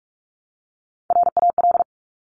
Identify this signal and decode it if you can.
Morse code, heard as a keyed tone, about 46 wpm, 700 Hz. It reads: RAL
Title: Top Three Letter Combinations in Words 45wpm